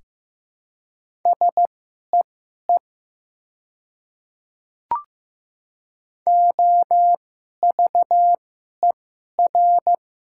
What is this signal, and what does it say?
Morse code, heard as a keyed tone, about 15 wpm, spaced as for 11 wpm, 690 Hz. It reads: SEE OVER
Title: 2x Character Spacing - Top 200 Words - 15wpm